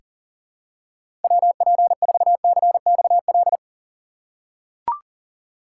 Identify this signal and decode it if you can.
Morse code, heard as a keyed tone, about 40 wpm, 700 Hz. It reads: WP4CXL
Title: USA Call Signs - 6 Characters - 3x Character Spacing - 40wpm